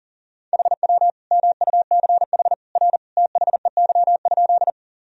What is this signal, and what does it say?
Morse code, heard as a keyed tone, about 40 wpm, 700 Hz. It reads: HW MUCH R THEY?
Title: Sentences from Top 200 Words and Abbreviations 40wpm